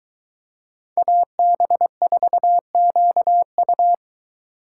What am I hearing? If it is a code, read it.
Morse code, heard as a keyed tone, about 23 wpm, 700 Hz. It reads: AB4QU